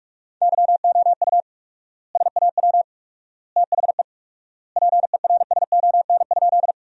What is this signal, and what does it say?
Morse code, heard as a keyed tone, about 46 wpm, 700 Hz. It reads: YOU SAW THE PERSON?
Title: 4x Word-Spacing - Sentences from Top 400 Words - 45wpm